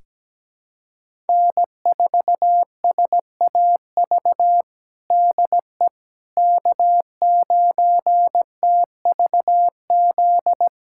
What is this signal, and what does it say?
Morse code, heard as a keyed tone, about 17 wpm, 700 Hz. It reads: N4SAV DE K9TVZ
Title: USA Callsign DE Callsign - 5 Characters - 17wpm